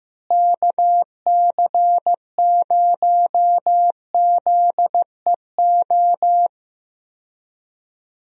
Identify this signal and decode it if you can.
Morse code, heard as a keyed tone, about 15 wpm, 700 Hz. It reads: KC0ZEO
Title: USA Call Signs - 6 Characters - Repeated 2x - 15wpm